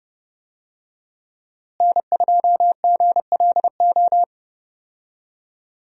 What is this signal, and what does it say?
Morse code, heard as a keyed tone, about 32 wpm, 700 Hz. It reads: N2GLO
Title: USA Call Signs - 5 Characters - Repeated 2x - 30wpm